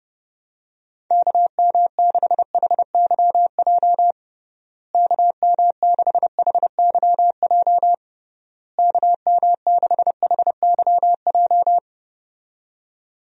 Morse code, heard as a keyed tone, about 30 wpm, 700 Hz. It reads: KM6HYJ KM6HYJ KM6HYJ
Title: USA Call Signs - 6 Characters - Repeated 3x - 30wpm